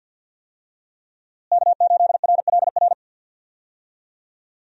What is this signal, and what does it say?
Morse code, heard as a keyed tone, about 46 wpm, 700 Hz. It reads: K8RLR
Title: USA Call Signs - 5 Characters - Repeated 3x - 50wpm